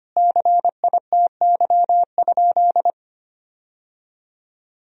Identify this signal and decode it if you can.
Morse code, heard as a keyed tone, about 25 wpm, 700 Hz. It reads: CITY?
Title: Sentences from Top 1000 Words - Review Part 1 - 25wpm